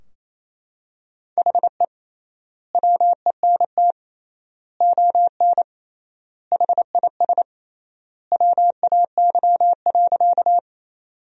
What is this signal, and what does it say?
Morse code, heard as a keyed tone, about 28 wpm, 700 Hz. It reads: HE WENT ON HIS WAY.
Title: 3x Word-Spacing - Sentences from Top 200 Words - 28wpm